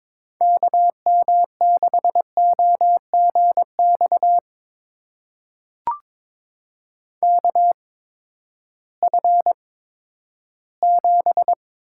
Morse code, heard as a keyed tone, about 22 wpm, spaced as for 13 wpm, 700 Hz. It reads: KM6OGX KF7
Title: USA Call Signs - 6 Characters - 8x Character Spacing - 22wpm